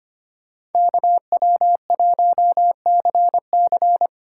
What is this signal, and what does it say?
Morse code, heard as a keyed tone, about 32 wpm, 700 Hz. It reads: KW1CC